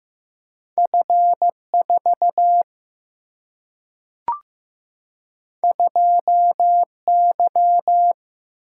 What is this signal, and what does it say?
Morse code, heard as a keyed tone, about 15 wpm, 700 Hz. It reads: F 4 2 Y
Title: Single-Letter Single-Number 15wpm